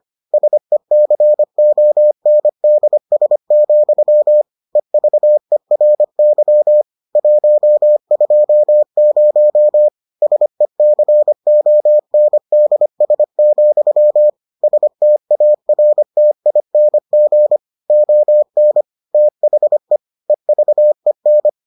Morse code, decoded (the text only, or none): SECONDS, EVERY 120 SECONDS, STARTING ON THE EVEN